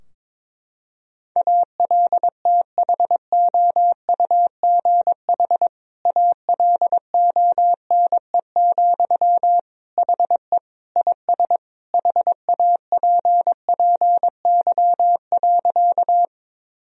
Morse code, heard as a keyed tone, about 22 wpm, 700 Hz. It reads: ALTHOUGH ALONE, HE IS HAPPY.
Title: Sentences from Top 1000 Words - Review Part 3 - 22wpm